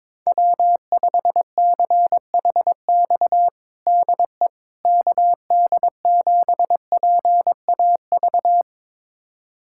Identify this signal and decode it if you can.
Morse code, heard as a keyed tone, about 22 wpm, 700 Hz. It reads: W5CHX DE KD7PAV
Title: USA Callsign DE Callsign 22wpm